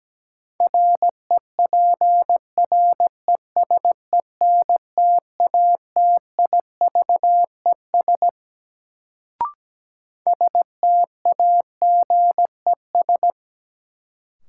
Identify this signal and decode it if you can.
Morse code, heard as a keyed tone, about 17 wpm, 700 Hz. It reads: REPRESENTATIVES STAGES
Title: Sets of 5 Words Encore - New Word Drill - 17wpm